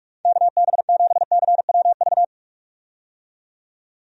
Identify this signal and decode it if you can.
Morse code, heard as a keyed tone, about 42 wpm, 700 Hz. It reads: KB7CWV
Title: USA Call Signs - 6 Characters - 45wpm